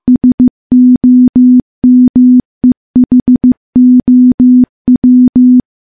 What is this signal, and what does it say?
Morse code, heard as a keyed tone, about 15 wpm, 250 Hz. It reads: SOMEHOW
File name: morse-training-15WPM